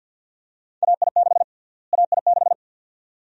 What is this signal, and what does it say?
Morse code, heard as a keyed tone, about 51 wpm, 700 Hz. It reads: AI6 AI6
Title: USA Call Sign Prefixes - Repeated 2x - 50wpm